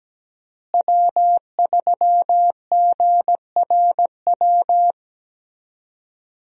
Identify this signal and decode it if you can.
Morse code, heard as a keyed tone, about 17 wpm, 700 Hz. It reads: W3GRW
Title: USA Call Signs - 5 Characters - Repeated 2x - 17wpm